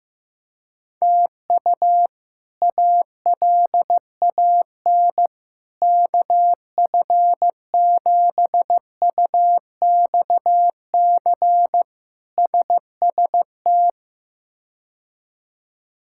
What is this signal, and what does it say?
Morse code, heard as a keyed tone, about 16 wpm, 700 Hz. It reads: TU ALAN KF7UXC SST
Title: K1USN SST QSO Practice - 15wpm